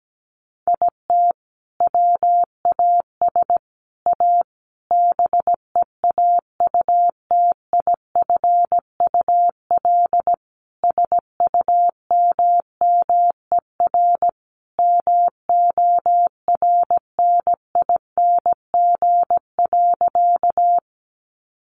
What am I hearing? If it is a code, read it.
Morse code, heard as a keyed tone, about 17 wpm, 700 Hz. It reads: IT WAS A BEAUTIFUL SUMMER MORNING.
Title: Sentences from Top 1100 Words 17wpm